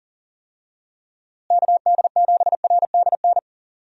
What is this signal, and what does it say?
Morse code, heard as a keyed tone, about 40 wpm, 700 Hz. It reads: KD7RDN
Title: USA Call Signs - 6 Characters - Repeated 3x - 40wpm